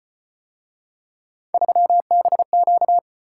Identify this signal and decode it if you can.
Morse code, heard as a keyed tone, about 35 wpm, 700 Hz. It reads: 3BQ